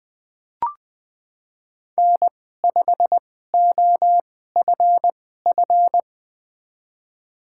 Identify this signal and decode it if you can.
Morse code, heard as a keyed tone, about 20 wpm, spaced as for 14 wpm, 700 Hz. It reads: N5OFF
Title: USA Call Signs - 2x Character Spacing - 20wpm